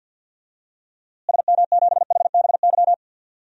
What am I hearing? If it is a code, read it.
Morse code, heard as a keyed tone, about 46 wpm, 700 Hz. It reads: SM7HBY
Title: DXCC-100 Call Signs - 6 Characters - Repeated 3x - 50wpm